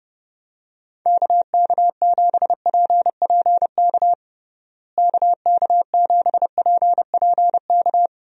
Morse code, heard as a keyed tone, about 30 wpm, 700 Hz. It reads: KK7PPK KK7PPK
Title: USA Call Signs - 6 Characters - Repeated 2x - 30wpm